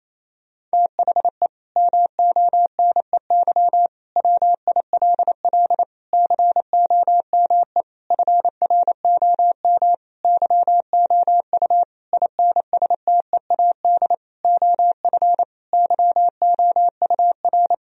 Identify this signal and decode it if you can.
Morse code, heard as a keyed tone, about 28 wpm, 700 Hz. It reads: THE MONEY WILL COME FROM YOU INSTEAD OF YOUR
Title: Sentences from Top 1000 Words - Review Part 4 - 28wpm